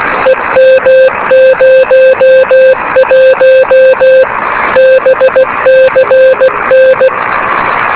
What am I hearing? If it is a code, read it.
Morse code, heard as a keyed tone, about 16 wpm, 520 Hz, with radio noise. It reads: EM01 BCN
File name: beacons / k5ab